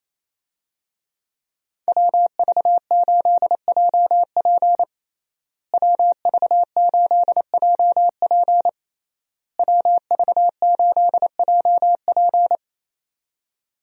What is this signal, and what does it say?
Morse code, heard as a keyed tone, about 27 wpm, 700 Hz. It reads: WV8JP WV8JP WV8JP
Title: USA Call Signs - 5 Characters - Repeated 3x - 28wpm